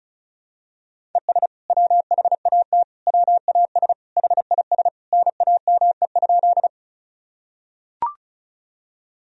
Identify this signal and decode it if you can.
Morse code, heard as a keyed tone, about 36 wpm, 700 Hz. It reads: ES WHAT WAS HIS NAME?
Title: Sentences from Top 200 Words and Abbreviations 35wpm